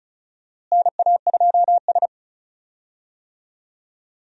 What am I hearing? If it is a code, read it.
Morse code, heard as a keyed tone, about 36 wpm, 700 Hz. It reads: NA2S
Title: USA Call Signs - 4 Characters - Repeated 3x - 35wpm